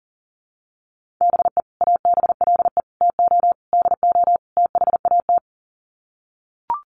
Morse code, heard as a keyed tone, about 37 wpm, 700 Hz. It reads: BE ABLE TO DO THAT
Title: Sets of 5 Words Encore2 40wpm